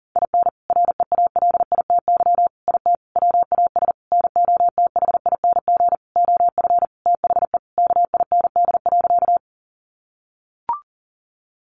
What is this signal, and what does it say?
Morse code, heard as a keyed tone, about 37 wpm, 700 Hz. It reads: IN REALITY IT WAS NOTHING OF THE KIND.
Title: Sentences from Top 1200 Words 40wpm